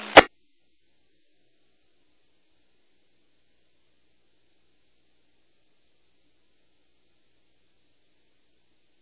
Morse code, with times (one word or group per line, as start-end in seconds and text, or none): none